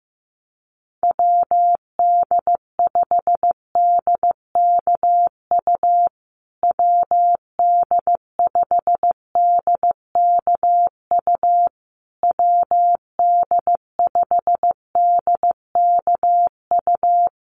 Morse code, heard as a keyed tone, about 15 wpm, 700 Hz. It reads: WD5DKU WD5DKU WD5DKU
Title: USA Call Signs Repeated 3x 15wpm